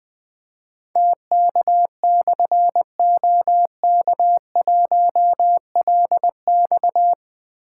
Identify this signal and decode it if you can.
Morse code, heard as a keyed tone, about 20 wpm, 700 Hz. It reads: TK/OK1LX
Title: Difficult Call Signs 20wpm